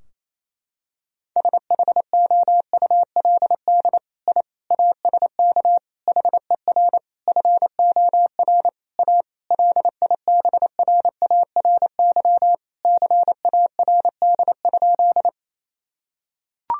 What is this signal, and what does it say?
Morse code, heard as a keyed tone, about 27 wpm, 700 Hz. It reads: SHOULD I ASK HER FOR A LIBRARY CARD?
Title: Sentences from Top 1000 Words - Review Part 4 - 28wpm